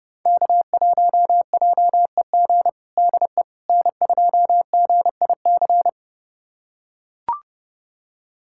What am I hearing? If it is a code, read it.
Morse code, heard as a keyed tone, about 30 wpm, 690 Hz. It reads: K1JEG DE N2GIC
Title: USA Callsign DE Callsign 30wpm